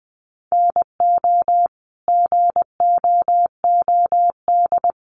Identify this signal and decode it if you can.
Morse code, heard as a keyed tone, about 20 wpm, 700 Hz. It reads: NO GOOD
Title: Sentences from Top 800 Words 20wpm